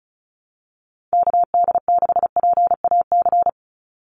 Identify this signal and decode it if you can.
Morse code, heard as a keyed tone, about 34 wpm, 700 Hz. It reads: KD6PAC
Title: USA Call Signs 35wpm